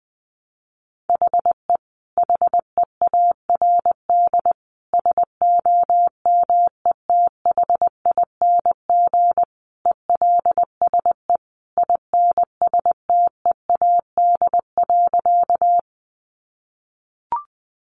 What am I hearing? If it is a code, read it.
Morse code, heard as a keyed tone, about 20 wpm, 700 Hz. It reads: HE HEARD SOMETHING ELSE INSTEAD.
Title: Sentences from Top 800 Words 20wpm